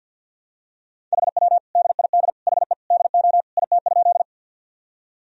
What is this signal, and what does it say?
Morse code, heard as a keyed tone, about 53 wpm, 700 Hz. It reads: HW DID HE DO IT?
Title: Sentences from Top 100 Words and Abbreviations 50wpm